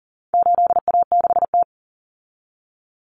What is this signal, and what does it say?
Morse code, heard as a keyed tone, about 39 wpm, 700 Hz. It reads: 8A6T
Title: Common Cut Numbers - 4-Digit 40wpm